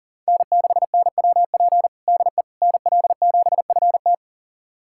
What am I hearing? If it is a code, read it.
Morse code, heard as a keyed tone, about 40 wpm, 700 Hz. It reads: N6NWP DE NL7FT
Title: USA Callsign DE Callsign - 5 Characters - 40wpm